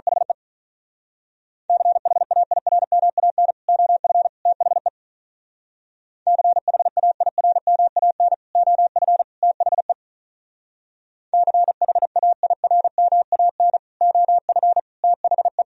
Morse code, heard as a keyed tone, about 42 wpm, 690 Hz. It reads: HE CHAIRMAN OF THE CHAIRMAN OF THE CHAIRMAN OF THE
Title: Speed-Racing - Sets of 3 Words 35wpm